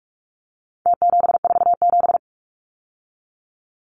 Morse code, heard as a keyed tone, about 46 wpm, 700 Hz. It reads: T747
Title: Numbers - Contest Style - 4-Digit 45wpm